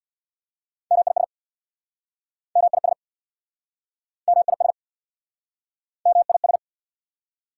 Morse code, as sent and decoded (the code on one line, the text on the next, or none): -- .. ... / -- .. ... / -- .. ... / -- .. ...
MIS MIS MIS MIS